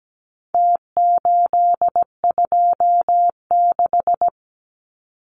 Morse code, heard as a keyed tone, about 17 wpm, 700 Hz. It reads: T826
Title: Numbers - Contest Style - 4-Digit 17wpm